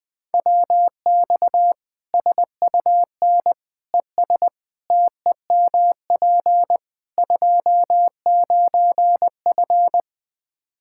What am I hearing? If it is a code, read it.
Morse code, heard as a keyed tone, about 21 wpm, 700 Hz. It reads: WX SUN ES TEMP 29F
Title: K1USN SST QSO Practice 20wpm